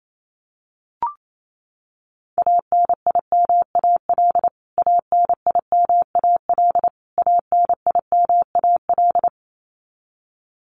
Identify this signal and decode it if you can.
Morse code, heard as a keyed tone, about 28 wpm, 700 Hz. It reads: ANIMAL ANIMAL ANIMAL
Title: Top 500 Words Repeated 28wpm